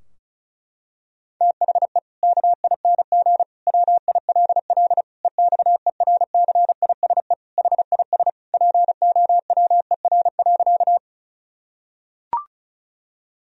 Morse code, heard as a keyed tone, about 36 wpm, 700 Hz. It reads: THE KING WILL EXERCISE HIS POWER.
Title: Sentences from Top 1000 Words - Review Part 4 - 35wpm